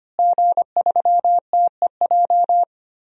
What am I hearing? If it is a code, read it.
Morse code, heard as a keyed tone, about 25 wpm, 700 Hz. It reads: G3TEJ
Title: DXCC-100 Call Signs - 5 Characters - Repeated 3x - 25wpm